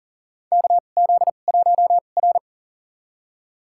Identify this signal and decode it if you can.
Morse code, heard as a keyed tone, about 40 wpm, spaced as for 29 wpm, 700 Hz. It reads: KZ1R
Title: USA Call Signs - 2x Character Spacing - 40wpm